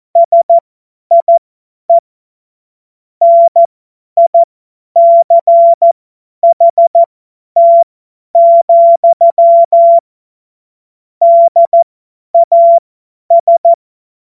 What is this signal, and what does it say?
Morse code, heard as a keyed tone, about 14 wpm, spaced as for 10 wpm, 680 Hz. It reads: SIE NICHT, DAS
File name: Morsekurs / Witze / Witz 15. November 2022 50 F70